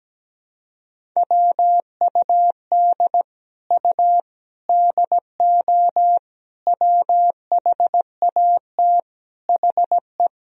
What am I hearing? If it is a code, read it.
Morse code, heard as a keyed tone, about 17 wpm, 700 Hz. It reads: WUD U DO WHAT HE
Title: Sentences from Top 100 Words and Abbreviations 17wpm